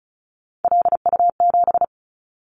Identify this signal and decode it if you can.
Morse code, heard as a keyed tone, about 34 wpm, 700 Hz. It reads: LU7
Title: Callsign Prefixes for Top 100 DXCC - 35wpm